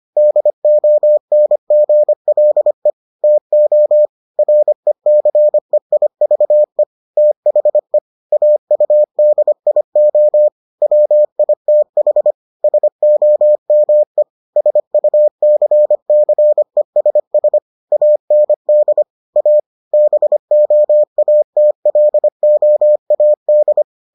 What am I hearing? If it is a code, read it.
Morse code, heard as a keyed tone, about 25 wpm, 600 Hz. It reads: DONGLE TO RECEIVE THE AUDIO WITH SOME SUCCESS AND A BOATLOAD